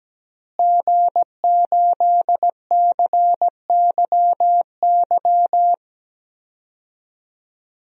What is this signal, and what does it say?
Morse code, heard as a keyed tone, about 17 wpm, 700 Hz. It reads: G8CYY
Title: DXCC-100 Call Signs - 5 Characters - Repeated 3x - 17wpm